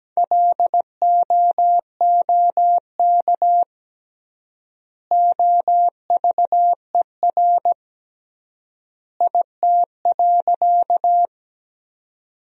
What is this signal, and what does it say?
Morse code, heard as a keyed tone, about 17 wpm, 700 Hz. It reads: LOOK OVER IT.
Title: 3x Word-Spacing - Sentences from Top 100 Words - 17wpm